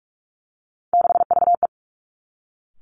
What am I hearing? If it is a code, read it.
Morse code, heard as a keyed tone, about 46 wpm, 700 Hz. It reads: 6VE
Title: Three Challenging Letter-Number Combinations 45wpm